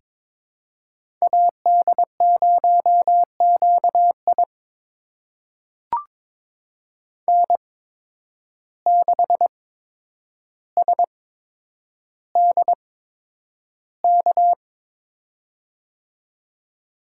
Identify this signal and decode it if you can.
Morse code, heard as a keyed tone, about 21 wpm, spaced as for 9 wpm, 700 Hz. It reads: AD0QI N6SDK
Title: USA Call Signs - 5 Characters - 8x Character Spacing - 22wpm